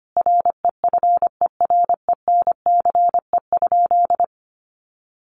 Morse code, heard as a keyed tone, about 25 wpm, 700 Hz. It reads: REFERENCE?
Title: Sentences from Top 1200 Words 25wpm